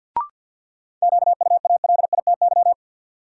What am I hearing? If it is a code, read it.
Morse code, heard as a keyed tone, about 52 wpm, 700 Hz. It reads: QUALITY
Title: Sentences from Top 1000 Words - Review Part 4 - New Word Drill - 50wpm